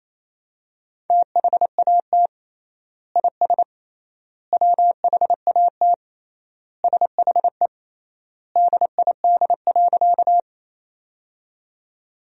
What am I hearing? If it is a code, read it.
Morse code, heard as a keyed tone, about 28 wpm, 700 Hz. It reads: THAT IS WHAT SHE DID.
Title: 3x Word-Spacing - Sentences from Top 100 Words - 28wpm